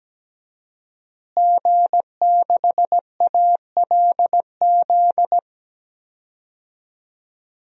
Morse code, heard as a keyed tone, about 17 wpm, 700 Hz. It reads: G6ALZ
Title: DXCC-100 Call Signs - 5 Characters - Repeated 2x - 17wpm